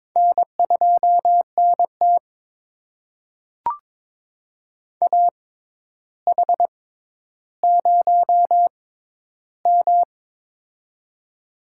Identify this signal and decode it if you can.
Morse code, heard as a keyed tone, about 22 wpm, spaced as for 11 wpm, 700 Hz. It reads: N2NT AH0M
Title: USA Call Signs - 4 Characters - 6x Character Spacing - 22wpm